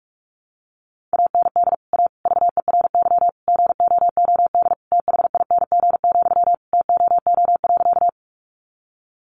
Morse code, heard as a keyed tone, about 44 wpm, 700 Hz. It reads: AND A VERY GOOD THING, TOO.
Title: Sentences from Top 200 Words 45wpm